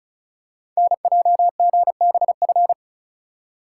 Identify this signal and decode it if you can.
Morse code, heard as a keyed tone, about 36 wpm, 690 Hz. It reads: NJGBF
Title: Code Groups - 5 Letters - 35wpm